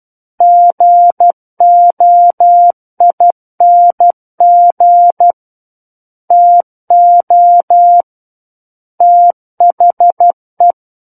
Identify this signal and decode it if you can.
Morse code, heard as a keyed tone, about 12 wpm, 700 Hz. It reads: GOING TO THE
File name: Book One / 03-Chapter 3 - 12 WPM